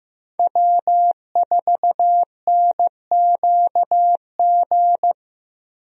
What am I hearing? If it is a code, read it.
Morse code, heard as a keyed tone, about 15 wpm, 700 Hz. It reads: W4NQG
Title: USA Call Signs - 5 Characters - Repeated 2x - 15wpm